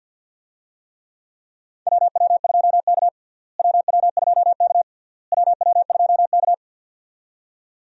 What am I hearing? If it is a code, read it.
Morse code, heard as a keyed tone, about 50 wpm, 700 Hz. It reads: WW2K WW2K WW2K
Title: USA Call Signs - 4 Characters - Repeated 3x - 50wpm